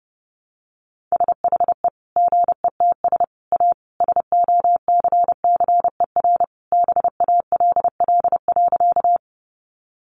Morse code, heard as a keyed tone, about 30 wpm, 700 Hz. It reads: SHE GETS A SOCCER BALL.
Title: Easy Phrases 30wpm